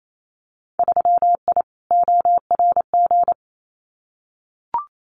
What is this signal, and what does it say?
Morse code, heard as a keyed tone, about 28 wpm, 700 Hz. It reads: 3I ORG
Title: Winter Field Day Exchanges Encore 28wpm